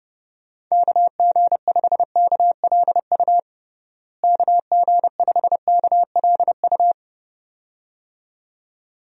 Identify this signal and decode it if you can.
Morse code, heard as a keyed tone, about 32 wpm, 700 Hz. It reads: KG5KLU KG5KLU
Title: USA Call Signs - 6 Characters - Repeated 2x - 30wpm